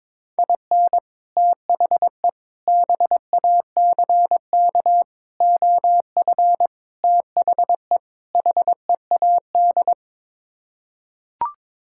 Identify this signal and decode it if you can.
Morse code, heard as a keyed tone, about 22 wpm, 700 Hz. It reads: IN THE BACK OF THE HEAD
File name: practice / 22wpm / Sets of 6 Words - 22wpm